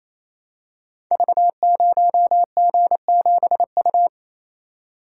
Morse code, heard as a keyed tone, about 28 wpm, 700 Hz. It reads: V0G7U
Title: Code Groups - 5 Letter:Numbers - 28wpm